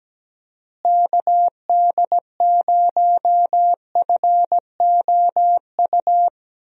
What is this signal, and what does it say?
Morse code, heard as a keyed tone, about 17 wpm, 700 Hz. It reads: KD0FOU
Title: USA Call Signs - 6 Characters - Repeated 2x - 17wpm